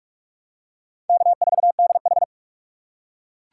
Morse code, heard as a keyed tone, about 45 wpm, 700 Hz. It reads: K4DH
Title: USA Call Signs - 4 Characters - 2x Character Spacing - 45wpm